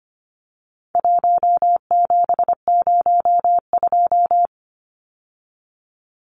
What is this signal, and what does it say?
Morse code, heard as a keyed tone, about 25 wpm, 700 Hz. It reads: 1702
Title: Numbers - 4-Digit 25wpm